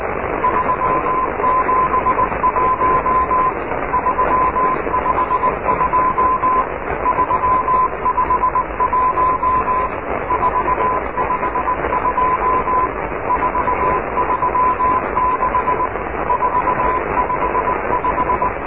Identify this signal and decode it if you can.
Morse code, heard as a keyed tone, about 18 wpm, 1000 Hz, with radio noise. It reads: 371 352 351 452 436 455
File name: oscar 7 de k5dze section 2